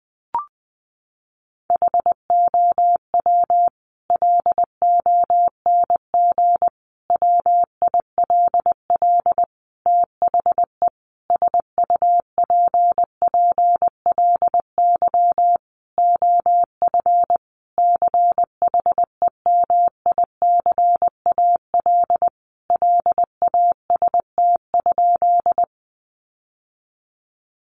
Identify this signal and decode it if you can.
Morse code, heard as a keyed tone, about 20 wpm, 700 Hz. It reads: HOW LONG WILL THE SUPPLY OF CHEMICAL LAST?
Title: Sentences from Top 1200 Words 20wpm